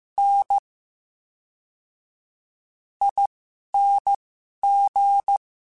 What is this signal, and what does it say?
Morse code, heard as a keyed tone, about 15 wpm, spaced as for 6 wpm, 800 Hz. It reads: NING
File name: Special INGs / Special INGs-mp3s / VerySlowToFaster Ing @10